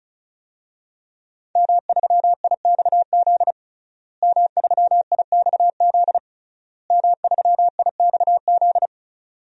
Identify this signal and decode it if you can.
Morse code, heard as a keyed tone, about 36 wpm, 700 Hz. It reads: M3IXZ M3IXZ M3IXZ
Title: DXCC-100 Call Signs - 5 Characters - Repeated 3x - 35wpm